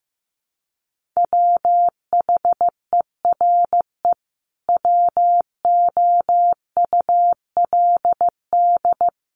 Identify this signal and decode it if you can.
Morse code, heard as a keyed tone, about 15 wpm, 700 Hz. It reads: WHERE WOULD
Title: Sentences from Top 1300 Words 15wpm